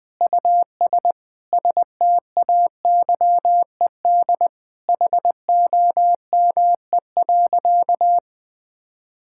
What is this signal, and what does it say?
Morse code, heard as a keyed tone, about 20 wpm, 700 Hz. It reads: US STAYED HOME.
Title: Sentences with 4-Word Phrases - Spoken Beginning - 20wpm